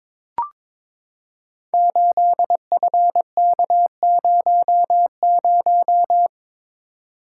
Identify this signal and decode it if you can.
Morse code, heard as a keyed tone, about 22 wpm, 700 Hz. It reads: 8FK00
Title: Code Groups - 5 Letter:Numbers - 22wpm